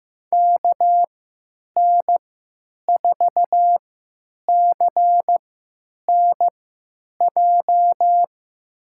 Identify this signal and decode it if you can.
Morse code, heard as a keyed tone, about 15 wpm, spaced as for 9 wpm, 700 Hz. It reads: KN4CNJ
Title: USA Call Signs - 6 Characters - 3x Character Spacing - 15wpm